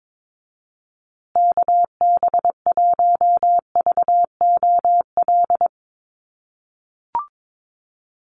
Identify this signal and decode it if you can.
Morse code, heard as a keyed tone, about 22 wpm, 700 Hz. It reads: KB1VOL
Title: USA Call Signs Repeated 3x 22wpm